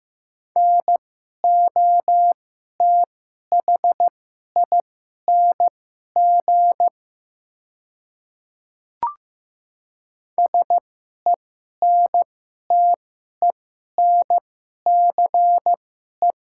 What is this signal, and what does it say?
Morse code, heard as a keyed tone, about 15 wpm, spaced as for 11 wpm, 690 Hz. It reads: NOTHING SENTENCE
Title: 2x Character Spacing - Top 500 Words - 15wpm